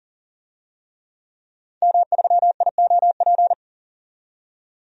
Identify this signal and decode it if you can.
Morse code, heard as a keyed tone, about 40 wpm, 700 Hz. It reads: M3IOP
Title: DXCC-100 Call Signs - 5 Characters - 40wpm